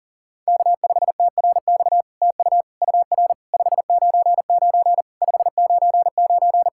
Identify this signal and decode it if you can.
Morse code, heard as a keyed tone, about 40 wpm, 700 Hz. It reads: K5TRX TU UR 599 599
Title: SOTA QSO Practice 40wpm